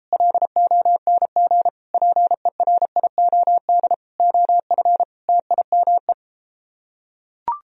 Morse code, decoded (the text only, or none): LONG PERIOD OF TIME